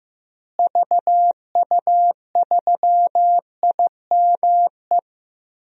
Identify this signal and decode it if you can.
Morse code, heard as a keyed tone, about 15 wpm, 700 Hz. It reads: VU3IME
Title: DXCC-100 Call Signs - 6 Characters - Repeated 3x - 15wpm